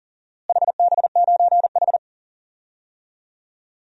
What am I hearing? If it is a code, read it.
Morse code, heard as a keyed tone, about 37 wpm, 700 Hz. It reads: HB9H
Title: DXCC-100 Call Signs - 4 Characters - Repeated 3x - 40wpm